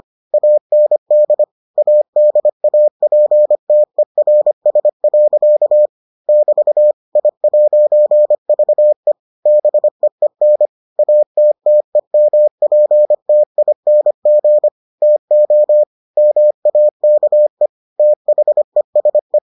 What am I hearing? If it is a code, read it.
Morse code, heard as a keyed tone, about 25 wpm, 600 Hz. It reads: AND ADAPTERS. <BT> I'VE BEEN ATTEMPTING TO MAKE THESE